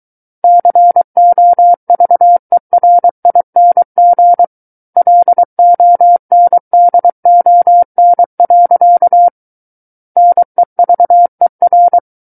Morse code, heard as a keyed tone, about 23 wpm, 700 Hz. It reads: COVERING LONDON. NEVER
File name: Book One / 14-Chapter 14 - 23 WPM